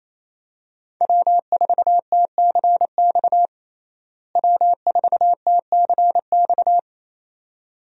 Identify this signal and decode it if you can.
Morse code, heard as a keyed tone, about 28 wpm, 700 Hz. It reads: W4TCX W4TCX
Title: USA Call Signs - 5 Characters - Repeated 2x - 28wpm